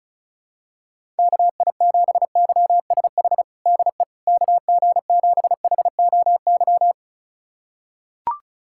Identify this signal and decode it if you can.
Morse code, heard as a keyed tone, about 36 wpm, 700 Hz. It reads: KI7YSH DE KG7HOY
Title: USA Callsign DE Callsign 35wpm